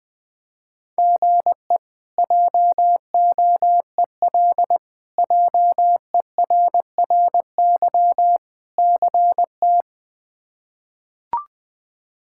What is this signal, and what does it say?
Morse code, heard as a keyed tone, about 20 wpm, 700 Hz. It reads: GE JOEL JERRY CT
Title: K1USN SST QSO Practice - 20wpm